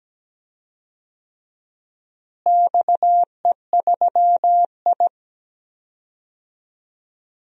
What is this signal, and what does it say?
Morse code, heard as a keyed tone, about 17 wpm, 700 Hz. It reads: XE3I
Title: DXCC-100 Call Signs - 4 Characters - Repeated 2x - 17wpm